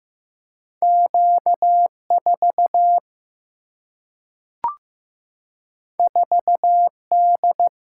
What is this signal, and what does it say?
Morse code, heard as a keyed tone, about 15 wpm, 700 Hz. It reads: Q4 4D
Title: Single-Letter Single-Number 15wpm